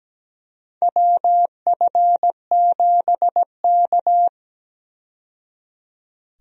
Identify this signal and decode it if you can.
Morse code, heard as a keyed tone, about 17 wpm, 700 Hz. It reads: WF7K
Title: USA Call Signs - 4 Characters - 17wpm